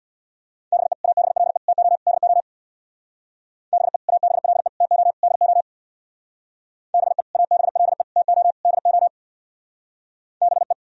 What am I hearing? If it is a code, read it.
Morse code, heard as a keyed tone, about 64 wpm, 700 Hz. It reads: BE ABLE TO DO BE ABLE TO DO BE ABLE TO DO BE